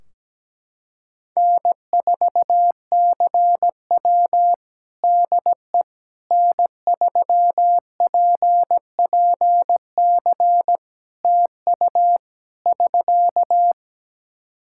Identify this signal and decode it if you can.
Morse code, heard as a keyed tone, about 17 wpm, 700 Hz. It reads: N4CW DE N3PPC TU <SK>
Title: Practice QSO - Exchange 4 - Ending - 17wpm